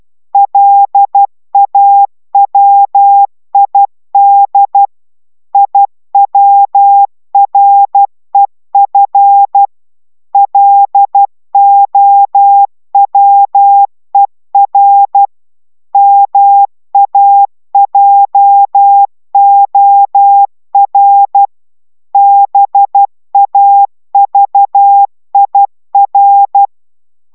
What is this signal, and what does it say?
Morse code, heard as a keyed tone, about 12 wpm, 800 Hz. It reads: LAWID IWREF LOWER MAJOR BAVIR